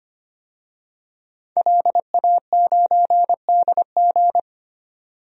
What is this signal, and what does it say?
Morse code, heard as a keyed tone, about 25 wpm, 700 Hz. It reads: LA9DG
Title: DXCC-100 Call Signs - 5 Characters - 25wpm